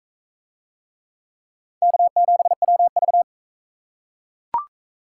Morse code, heard as a keyed tone, about 42 wpm, 700 Hz. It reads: K7WV